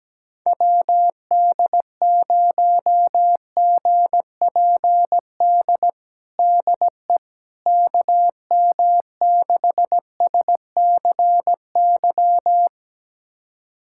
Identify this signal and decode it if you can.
Morse code, heard as a keyed tone, about 17 wpm, 700 Hz. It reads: WD0GPD DE KM6SCY
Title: USA Callsign DE Callsign 17wpm